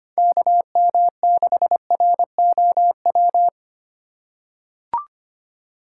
Morse code, heard as a keyed tone, about 25 wpm, 700 Hz. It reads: KM6ROW
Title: USA Call Signs - 3x Character Spacing - 25wpm